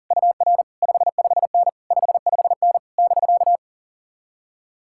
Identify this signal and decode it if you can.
Morse code, heard as a keyed tone, about 48 wpm, 700 Hz. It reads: UR 55N 55N <BK>
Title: Speed-Racing Common QSO Phrases - 40wpm